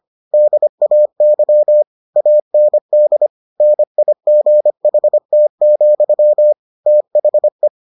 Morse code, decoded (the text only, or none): DAY AND NIGHT, THE